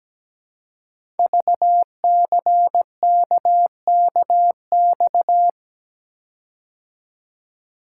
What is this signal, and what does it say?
Morse code, heard as a keyed tone, about 17 wpm, 700 Hz. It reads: VCKKX
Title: Code Groups - 5 Letters - 17wpm